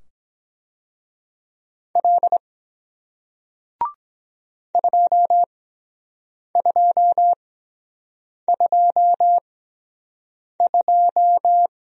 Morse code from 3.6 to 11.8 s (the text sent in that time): 2222